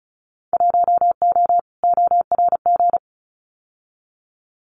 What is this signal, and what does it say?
Morse code, heard as a keyed tone, about 34 wpm, 700 Hz. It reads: 1O ORG
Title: Winter Field Day Exchanges 35wpm